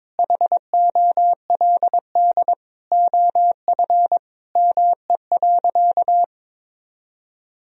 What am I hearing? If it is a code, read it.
Morse code, heard as a keyed tone, about 22 wpm, 700 Hz. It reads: HOLD OF ME.
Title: Sentences from Top 1000 Words - Review Part 2 - 22wpm